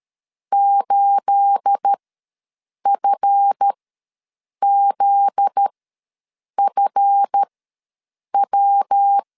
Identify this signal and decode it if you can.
Morse code, heard as a keyed tone, about 13 wpm, spaced as for 7 wpm, 800 Hz. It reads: 8FZFW
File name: LEZIONI / 10 SIGNS / 10A SLASH ws 12wpm 3443